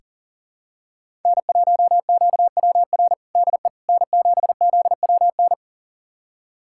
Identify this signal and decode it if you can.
Morse code, heard as a keyed tone, about 37 wpm, 700 Hz. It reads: N1QWR DE N7ZWN
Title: USA Callsign DE Callsign - 5 Characters - 40wpm